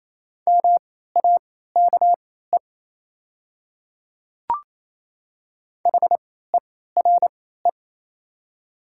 Morse code, heard as a keyed tone, about 28 wpm, spaced as for 16 wpm, 700 Hz. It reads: MAKE HERE
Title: 3x Character Spacing - Top 200 Words - 28wpm